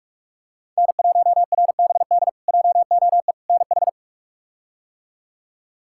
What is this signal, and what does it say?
Morse code, heard as a keyed tone, about 42 wpm, 700 Hz. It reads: N1RBD JOE NH
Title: K1USN SST QSO Practice - 45wpm